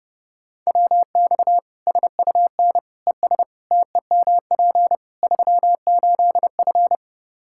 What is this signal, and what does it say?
Morse code, heard as a keyed tone, about 30 wpm, 700 Hz. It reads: WX SUN ES TEMP 38F
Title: Practice QSO - Exchange 2 - Info - 30wpm